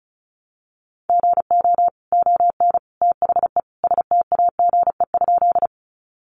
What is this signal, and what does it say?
Morse code, heard as a keyed tone, about 34 wpm, 700 Hz. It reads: GO ON THE STAGE?
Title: Sentences from Top 1200 Words 35wpm